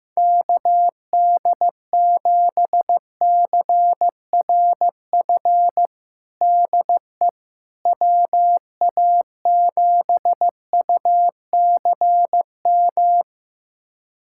Morse code, heard as a keyed tone, about 16 wpm, 700 Hz. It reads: KD7CRF DE WA7UCM
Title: USA Callsign DE Callsign - 6 Characters - 15wpm